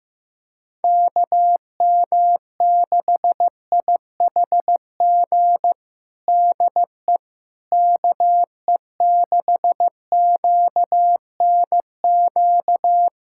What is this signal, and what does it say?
Morse code, heard as a keyed tone, about 15 wpm, 700 Hz. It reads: KM6IHG DE KE6QNQ
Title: USA Callsign DE Callsign - 6 Characters - 15wpm